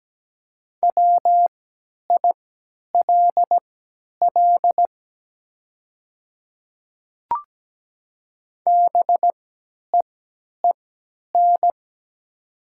Morse code, heard as a keyed tone, about 17 wpm, spaced as for 10 wpm, 700 Hz. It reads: WILL BEEN
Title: 3x Character Spacing - Top 100 Words - 17wpm